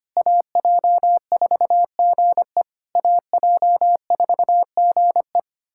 Morse code, heard as a keyed tone, about 25 wpm, 700 Hz. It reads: AJ4GE AJ4GE
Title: SOTA QSO Practice 25wpm